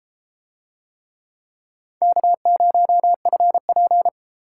Morse code, heard as a keyed tone, about 33 wpm, 700 Hz. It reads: K0FP